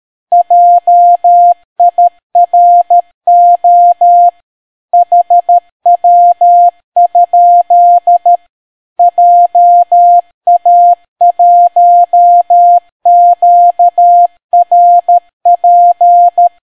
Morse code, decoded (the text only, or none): JIRO HW? JA1QRP